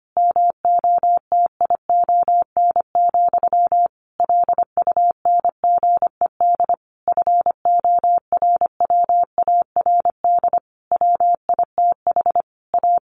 Morse code, heard as a keyed tone, about 25 wpm, 700 Hz. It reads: MOTION, LUNGED FORWARD WITH A
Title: A Sound Of Thunder 25wpm